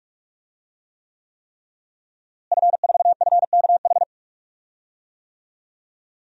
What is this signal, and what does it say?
Morse code, heard as a keyed tone, about 46 wpm, 700 Hz. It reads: F4FKH
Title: DXCC-100 Call Signs - 5 Characters - Repeated 3x - 45wpm